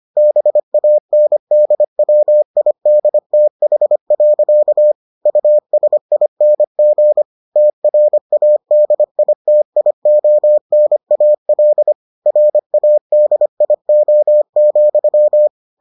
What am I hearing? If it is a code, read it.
Morse code, heard as a keyed tone, about 25 wpm, 600 Hz. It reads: BANDWIDTH. USING TRADITIONAL RADIO,